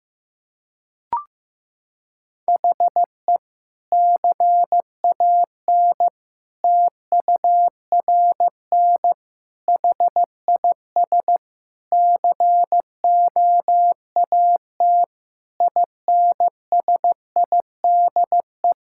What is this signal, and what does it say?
Morse code, heard as a keyed tone, about 14 wpm, 700 Hz. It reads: HE CAN TURN HIS COAT INSIDE
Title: Sentences from Top 1600 Words 15wpm